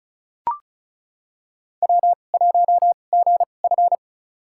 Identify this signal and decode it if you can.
Morse code, heard as a keyed tone, about 34 wpm, spaced as for 25 wpm, 700 Hz. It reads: W1GF
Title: USA Call Signs - 4 Characters - 2x Character Spacing - 35wpm